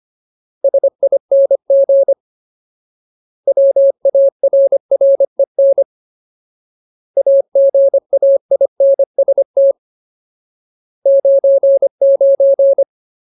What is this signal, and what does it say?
Morse code, heard as a keyed tone, about 25 wpm, 550 Hz. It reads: SING WARREN AGAINST 99
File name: Ultimate Word List / 25WPM Ultimate